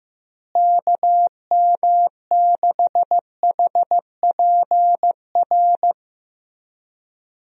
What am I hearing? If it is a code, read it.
Morse code, heard as a keyed tone, about 15 wpm, 700 Hz. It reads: KM6HPR
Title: USA Call Signs - 6 Characters - Repeated 3x - 15wpm